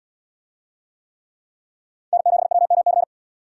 Morse code, heard as a keyed tone, about 71 wpm, 700 Hz. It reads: N7WMC